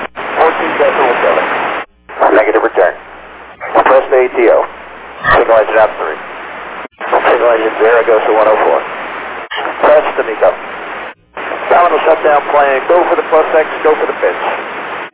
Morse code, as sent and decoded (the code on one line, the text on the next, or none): none